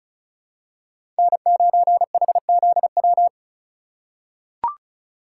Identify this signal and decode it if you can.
Morse code, heard as a keyed tone, about 36 wpm, 700 Hz. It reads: N9HZW
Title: USA Call Signs - 5 Characters - 3x Character Spacing - 35wpm